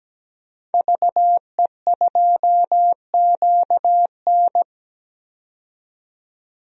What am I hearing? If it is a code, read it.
Morse code, heard as a keyed tone, about 17 wpm, 700 Hz. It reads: VE2QN
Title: DXCC-100 Call Signs - 5 Characters - 17wpm